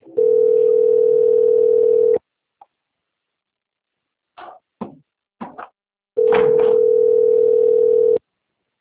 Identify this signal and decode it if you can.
Ringback tone (North America), 440 + 480 Hz.